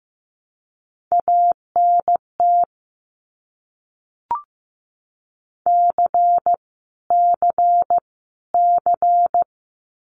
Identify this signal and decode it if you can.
Morse code, heard as a keyed tone, about 15 wpm, 700 Hz. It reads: ANT C C C
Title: Abbreviations Repeated 3x - 15wpm